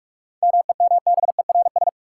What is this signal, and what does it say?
Morse code, heard as a keyed tone, about 43 wpm, 700 Hz. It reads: MEMBERS
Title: Sets of 3 Words - New Word Drill - 45wpm